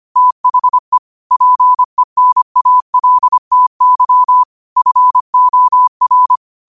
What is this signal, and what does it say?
Morse code, heard as a keyed tone, about 25 wpm, 1000 Hz. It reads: THE PENALTY FOR
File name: qotdfiles / 25wpm